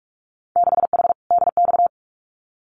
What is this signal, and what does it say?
Morse code, heard as a keyed tone, about 44 wpm, 700 Hz. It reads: 6H DX
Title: Winter Field Day Exchanges Encore 45wpm